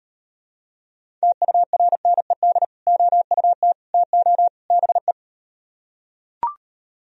Morse code, heard as a keyed tone, about 39 wpm, 700 Hz. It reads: TURNED OUT TO BE